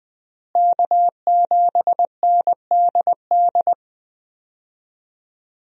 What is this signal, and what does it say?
Morse code, heard as a keyed tone, about 21 wpm, 700 Hz. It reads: K7NDD
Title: USA Call Signs - 5 Characters - Repeated 3x - 20wpm